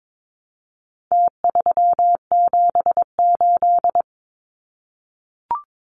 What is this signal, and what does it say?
Morse code, heard as a keyed tone, about 22 wpm, 700 Hz. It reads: T378
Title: Numbers - Contest Style - 4-Digit 22wpm